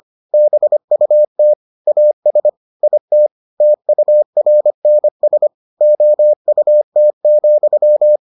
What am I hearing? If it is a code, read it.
Morse code, heard as a keyed tone, about 25 wpm, 600 Hz. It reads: BUT AS IT TURNS OUT,